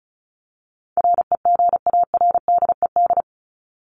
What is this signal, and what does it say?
Morse code, heard as a keyed tone, about 34 wpm, 700 Hz. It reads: REGARDED
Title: ED Suffix - Top 100 Words with Base Words 35wpm